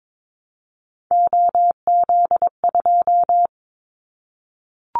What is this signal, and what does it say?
Morse code, heard as a keyed tone, about 22 wpm, 700 Hz. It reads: OZ2
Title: Callsign Prefixes for Top 50 DXCC - 22wpm